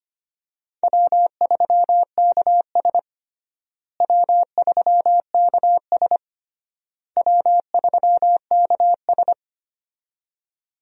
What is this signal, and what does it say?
Morse code, heard as a keyed tone, about 25 wpm, 700 Hz. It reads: W3KS W3KS W3KS
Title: USA Call Signs - 4 Characters - Repeated 3x - 25wpm